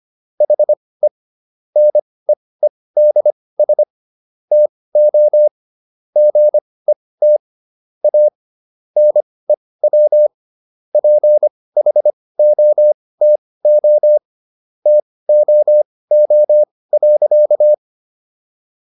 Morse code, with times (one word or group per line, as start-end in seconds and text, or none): 0.4-1.1 HE
1.8-3.8 NEEDS
4.5-5.5 TO
6.2-7.4 GET
8.0-8.3 A
9.0-10.3 NEW
10.9-14.2 PHOTO
14.9-17.7 TOO.